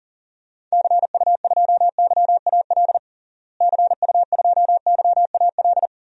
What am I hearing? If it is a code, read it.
Morse code, heard as a keyed tone, about 40 wpm, 700 Hz. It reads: CU2YAL CU2YAL
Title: DXCC-100 Call Signs - 6 Characters - Repeated 2x - 40wpm